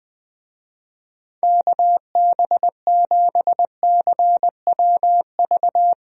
Morse code, heard as a keyed tone, about 20 wpm, 700 Hz. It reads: KB7CWV
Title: USA Call Signs - 6 Characters - 20wpm